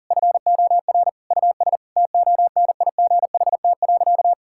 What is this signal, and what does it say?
Morse code, heard as a keyed tone, about 40 wpm, 700 Hz. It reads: FOR US TONIGHT.
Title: Sentences from Top 1000 Words - Review Part 5 - 40wpm